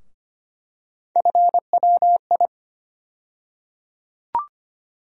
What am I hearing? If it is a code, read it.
Morse code, heard as a keyed tone, about 25 wpm, 700 Hz. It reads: FWI
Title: USA Call Sign Postfixes - 3 Letters 25wpm